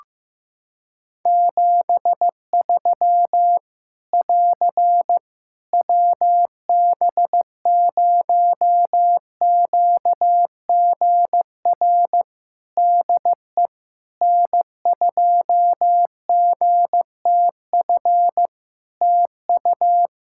Morse code, heard as a keyed tone, about 15 wpm, 700 Hz. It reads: 73 <AR> WB0QGR DE N2GTF TU
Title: Practice QSO - Exchange 4 - Ending - 15wpm